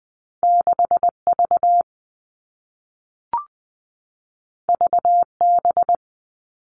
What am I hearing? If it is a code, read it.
Morse code, heard as a keyed tone, about 20 wpm, 700 Hz. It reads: 6 V V B
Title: Two Challenging Letter-Number Combinations 20wpm